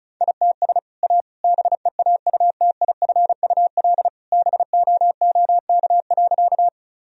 Morse code, heard as a keyed tone, about 36 wpm, 700 Hz. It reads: ITS A BEAUTIFUL BOOK.
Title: Sentences from Top 1000 Words - Review Part 1 - 35wpm